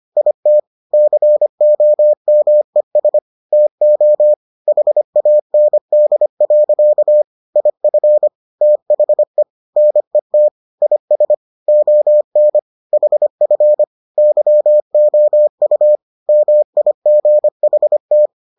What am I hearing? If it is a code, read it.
Morse code, heard as a keyed tone, about 25 wpm, 600 Hz. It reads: IT COMES TO HAND. IF THE NET IS ON HF YOU MIGHT